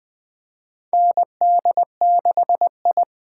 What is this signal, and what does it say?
Morse code, heard as a keyed tone, about 20 wpm, 700 Hz. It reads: ND6I
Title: USA Call Signs - 4 Characters - 3x Character Spacing - 20wpm